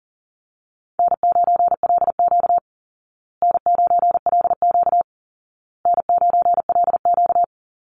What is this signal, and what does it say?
Morse code, heard as a keyed tone, about 40 wpm, 700 Hz. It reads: N9LQ N9LQ N9LQ
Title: Easy Call Signs Repeated 3x - 40wpm